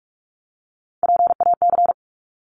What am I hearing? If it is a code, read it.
Morse code, heard as a keyed tone, about 46 wpm, 700 Hz. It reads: PAC
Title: ARRL Sections 45wpm